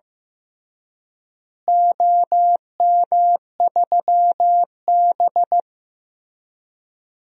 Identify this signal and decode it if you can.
Morse code, heard as a keyed tone, about 15 wpm, 700 Hz. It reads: OM3B
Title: DXCC-100 Call Signs - 4 Characters - Repeated 3x - 15wpm